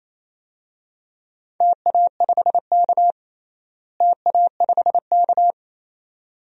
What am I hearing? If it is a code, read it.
Morse code, heard as a keyed tone, about 28 wpm, 700 Hz. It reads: TA5K TA5K
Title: DXCC-100 Call Signs - 4 Characters - Repeated 2x - 28wpm